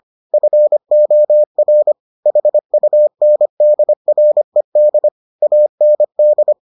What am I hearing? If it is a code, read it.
Morse code, heard as a keyed tone, about 25 wpm, 600 Hz. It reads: FOR HUNDRED AND